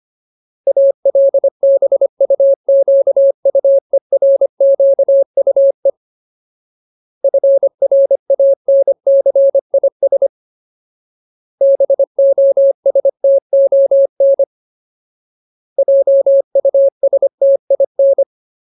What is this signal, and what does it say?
Morse code, heard as a keyed tone, about 25 wpm, 550 Hz. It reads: ALBUQUERQUE FRANCIS BOSTON JUSTIN
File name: Ultimate Word List / 25WPM Ultimate